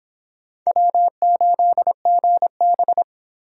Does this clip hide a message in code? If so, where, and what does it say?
Yes: Morse code from 0.7 to 3.0 s, heard as a keyed tone, about 25 wpm, 700 Hz. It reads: W8GB